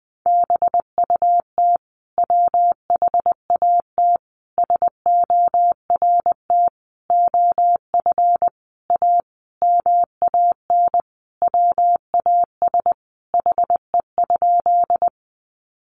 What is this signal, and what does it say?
Morse code, heard as a keyed tone, about 20 wpm, 700 Hz. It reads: BUT WHAT SORT OF A MAN WAS HE?
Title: Sentences from Top 1300 Words 20wpm